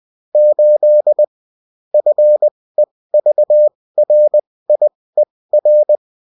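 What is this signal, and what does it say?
Morse code, heard as a keyed tone, about 20 wpm, spaced as for 16 wpm, 600 Hz. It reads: 8 FEVRIER
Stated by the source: Morse code bulletin of the UFT.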